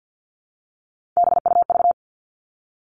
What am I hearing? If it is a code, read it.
Morse code, heard as a keyed tone, about 52 wpm, 700 Hz. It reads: BUV
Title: All Cut Numbers - 3-Digit 50wpm